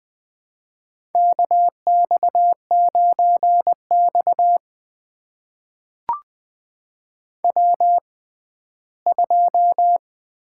Morse code, heard as a keyed tone, about 20 wpm, spaced as for 14 wpm, 700 Hz. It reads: KX9X W2
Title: USA Call Signs - 6x Character Spacing - 20wpm